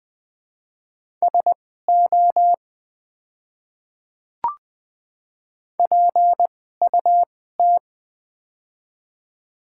Morse code, heard as a keyed tone, about 20 wpm, spaced as for 14 wpm, 700 Hz. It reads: SO PUT
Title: 2x Character Spacing - Top 200 Words - 20wpm